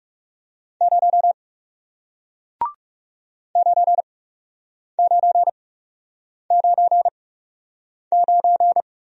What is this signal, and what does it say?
Morse code, heard as a keyed tone, about 40 wpm, spaced as for 8 wpm, 700 Hz. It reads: 0 9999